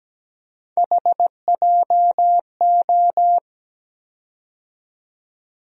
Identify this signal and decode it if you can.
Morse code, heard as a keyed tone, about 17 wpm, 700 Hz. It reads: HJO
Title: USA Call Sign Postfixes - 3 Letters 17wpm